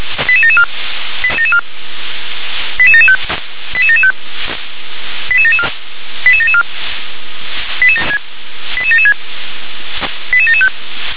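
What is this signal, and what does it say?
Five-tone selective call (ZVEI), heard as quick runs of five tones, tones of about 70 ms. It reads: 88774 88775 88775 88774 88774 88776 88775
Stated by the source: ZVEI five-tone selective call received on SDR.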